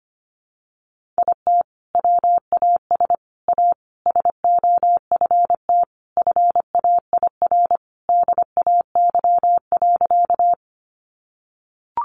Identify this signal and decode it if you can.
Morse code, heard as a keyed tone, about 25 wpm, 700 Hz. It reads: IT WAS A SOFT FAIR DAY.
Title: Sentences from Top 1500 Words 25wpm